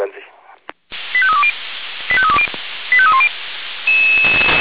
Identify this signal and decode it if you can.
Five-tone selective call (ZVEI), heard as quick runs of five tones, tones of about 70 ms, with speech. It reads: 85320 85320 85310